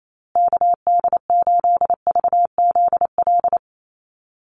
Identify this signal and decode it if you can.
Morse code, heard as a keyed tone, about 28 wpm, 700 Hz. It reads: KD8VZL
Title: USA Call Signs Repeated 3x 28wpm